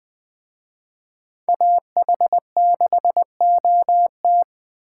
Morse code, heard as a keyed tone, about 24 wpm, 700 Hz. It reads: AH6OT
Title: Speed Racing - USA 2x2 Call Signs 20wpm